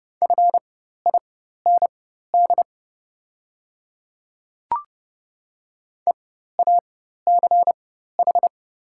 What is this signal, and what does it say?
Morse code, heard as a keyed tone, about 30 wpm, spaced as for 14 wpm, 700 Hz. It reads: FIND EACH
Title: 4x Character Spacing - Top 100 Words - 30wpm